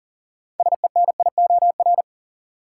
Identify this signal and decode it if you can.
Morse code, heard as a keyed tone, about 40 wpm, 700 Hz. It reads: SENIOR
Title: Sets of 5 Words Encore - New Word Drill - 40wpm